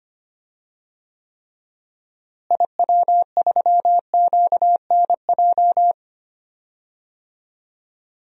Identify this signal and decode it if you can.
Morse code, heard as a keyed tone, about 25 wpm, 700 Hz. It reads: IW3QNJ
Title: DXCC-100 Call Signs - 6 Characters - Repeated 3x - 25wpm